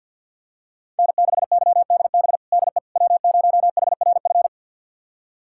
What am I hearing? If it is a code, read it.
Morse code, heard as a keyed tone, about 52 wpm, 690 Hz. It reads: N6YDB DE W0HRF
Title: USA Callsign DE Callsign - 5 Characters - 50wpm